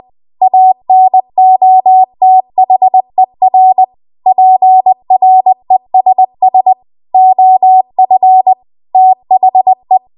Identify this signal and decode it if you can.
Morse code, heard as a keyed tone, about 20 wpm, 750 Hz. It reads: ANOTHER PRESS OF THE
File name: Archive / 20 WPM / 20WPM